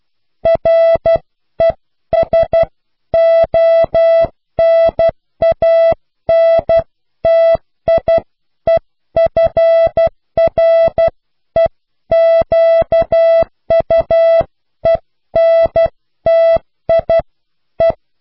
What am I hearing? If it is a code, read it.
Morse code, heard as a keyed tone, about 12 wpm, spaced as for 11 wpm, 660 Hz. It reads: RESONANTIEFREQUENTIE